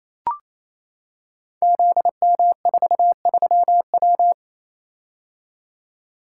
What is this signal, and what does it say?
Morse code, heard as a keyed tone, about 28 wpm, 700 Hz. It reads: ZM43W
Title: Code Groups - 5 Letter:Numbers - 28wpm